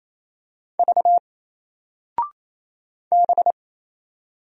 Morse code, heard as a keyed tone, about 27 wpm, 700 Hz. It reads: V B
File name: Single-Letter-Number-28wpm